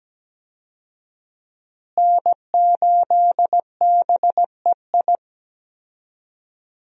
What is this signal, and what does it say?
Morse code, heard as a keyed tone, about 17 wpm, 700 Hz. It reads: N8BEI
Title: USA Call Signs - 5 Characters - Repeated 3x - 17wpm